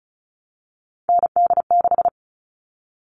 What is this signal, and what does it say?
Morse code, heard as a keyed tone, about 34 wpm, 700 Hz. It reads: ND6
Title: All Cut Numbers - 3-Digit 35wpm